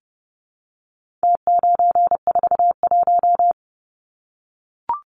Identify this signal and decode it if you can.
Morse code, heard as a keyed tone, about 30 wpm, 700 Hz. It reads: T941
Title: Numbers - Contest Style - 4-Digit 30wpm